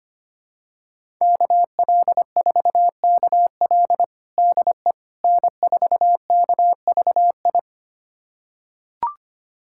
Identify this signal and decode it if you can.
Morse code, heard as a keyed tone, about 25 wpm, 700 Hz. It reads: KL4KL DE N4KVI
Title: USA Callsign DE Callsign - 5 Characters - 25wpm